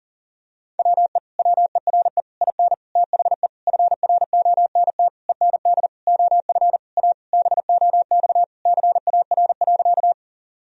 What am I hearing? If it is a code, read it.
Morse code, heard as a keyed tone, about 40 wpm, 700 Hz. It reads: WE WERE IN THE FRONT END OF A BOX CAR.
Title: Sentences from Top 1000 Words - Review Part 2 - 40wpm